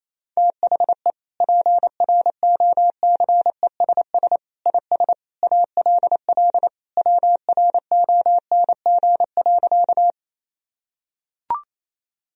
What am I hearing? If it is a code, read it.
Morse code, heard as a keyed tone, about 28 wpm, 700 Hz. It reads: THE PROCESS IS ALL WRONG.
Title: Sentences from Top 1000 Words - Review Part 3 - 28wpm